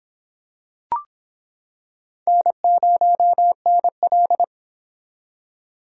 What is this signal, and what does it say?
Morse code, heard as a keyed tone, about 26 wpm, 700 Hz. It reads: N0NL